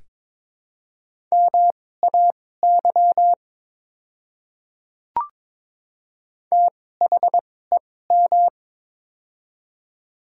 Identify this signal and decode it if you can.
Morse code, heard as a keyed tone, about 22 wpm, spaced as for 16 wpm, 700 Hz. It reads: MAY THEM
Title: 2x Character Spacing - Top 100 Words - 22wpm